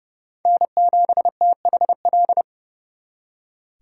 Morse code, heard as a keyed tone, about 36 wpm, 700 Hz. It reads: N7THL
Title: Speed Racing - USA 1x3 Call Signs 30wpm